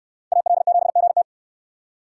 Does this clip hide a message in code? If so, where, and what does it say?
Yes: Morse code from 0.3 to 1.2 s, heard as a keyed tone, about 68 wpm, 700 Hz. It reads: AL7GT